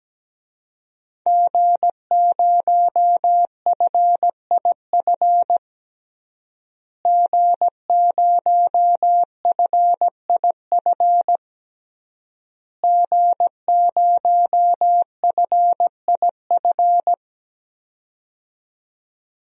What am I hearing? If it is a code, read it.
Morse code, heard as a keyed tone, about 17 wpm, 700 Hz. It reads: G0FIF G0FIF G0FIF
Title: DXCC-100 Call Signs - 5 Characters - Repeated 3x - 17wpm